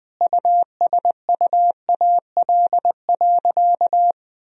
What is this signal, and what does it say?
Morse code, heard as a keyed tone, about 20 wpm, 700 Hz. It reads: USUAL.
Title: Sentences from Top 1000 Words - Review Part 2 - 20wpm